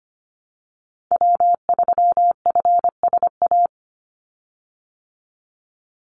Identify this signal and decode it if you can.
Morse code, heard as a keyed tone, about 25 wpm, 700 Hz. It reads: W3FSA
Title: Call Signs Repeated 3x - 25wpm